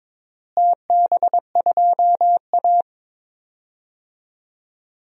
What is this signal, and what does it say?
Morse code, heard as a keyed tone, about 22 wpm, 700 Hz. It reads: TB2A
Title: DXCC-100 Call Signs - 4 Characters - Repeated 3x - 22wpm